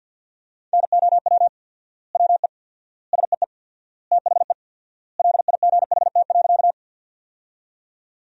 Morse code, heard as a keyed tone, about 53 wpm, 700 Hz. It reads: NOW WE SEE THE LIGHT.
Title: 4x Word-Spacing - Sentences from Top 200 Words - 50wpm